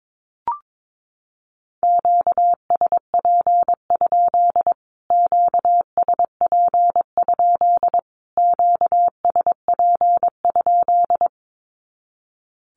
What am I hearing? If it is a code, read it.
Morse code, heard as a keyed tone, about 22 wpm, 700 Hz. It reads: QSP? QSP? QSP?
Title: Q-Codes Repeated 3x 22wpm